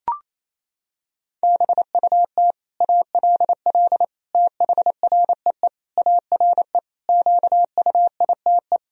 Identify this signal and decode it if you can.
Morse code, heard as a keyed tone, about 27 wpm, 700 Hz. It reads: BUT ALL THREE ARE QUITE
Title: Sentences from Top 1000 Words - Review Part 3 - 28wpm